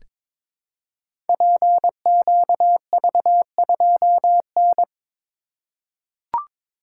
Morse code, heard as a keyed tone, about 22 wpm, 700 Hz. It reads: PQV2N
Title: Code Groups - 5 Letter:Numbers - 22wpm